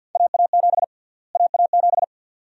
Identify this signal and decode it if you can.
Morse code, heard as a keyed tone, about 50 wpm, 700 Hz. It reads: AA7 AA7
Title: USA Call Sign Prefixes - Repeated 2x - 50wpm